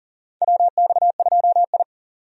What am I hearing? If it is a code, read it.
Morse code, heard as a keyed tone, about 40 wpm, 700 Hz. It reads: WX2I
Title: USA Call Signs - 4 Characters - 40wpm